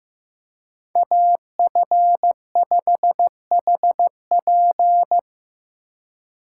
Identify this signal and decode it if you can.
Morse code, heard as a keyed tone, about 18 wpm, 700 Hz. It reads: AF5HP
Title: Speed Racing - USA Call Signs 15wpm